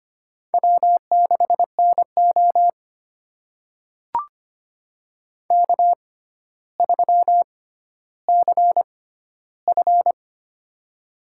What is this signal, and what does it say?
Morse code, heard as a keyed tone, about 25 wpm, spaced as for 13 wpm, 700 Hz. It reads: W6NO K3CF
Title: USA Call Signs - 6x Character Spacing - 25wpm